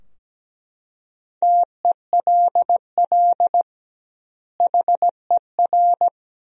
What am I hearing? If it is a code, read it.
Morse code, heard as a keyed tone, about 17 wpm, 700 Hz. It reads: TELL HER
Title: 2x Word-Spacing - Sentences from Top 300 Words - 17wpm